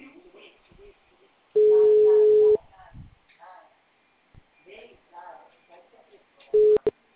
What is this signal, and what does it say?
Ringback tone (Europe, 425 Hz), with speech.